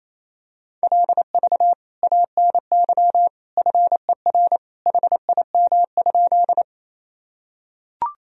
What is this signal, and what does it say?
Morse code, heard as a keyed tone, about 27 wpm, 700 Hz. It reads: LV ANY FER HIM?
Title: Sentences from Top 400 Words and Abbreviations 28wpm